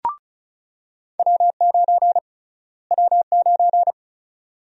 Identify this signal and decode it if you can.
Morse code, heard as a keyed tone, about 36 wpm, 700 Hz. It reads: W9 W9
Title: USA Call Sign Prefixes - Repeated 2x - 35wpm